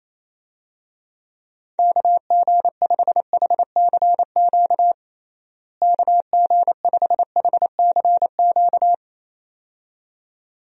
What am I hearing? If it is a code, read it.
Morse code, heard as a keyed tone, about 28 wpm, 700 Hz. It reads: KG5HCQ KG5HCQ
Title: USA Call Signs - 6 Characters - Repeated 2x - 28wpm